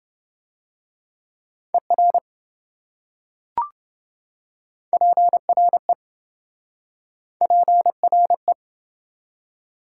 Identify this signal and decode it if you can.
Morse code, heard as a keyed tone, about 26 wpm, 700 Hz. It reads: ER PRE PRE